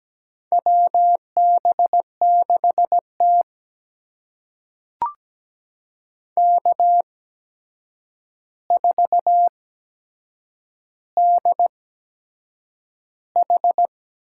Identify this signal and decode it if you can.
Morse code, heard as a keyed tone, about 17 wpm, spaced as for 7 wpm, 700 Hz. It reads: WB6T K4DH
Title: USA Call Signs - 4 Characters - 8x Character Spacing - 17wpm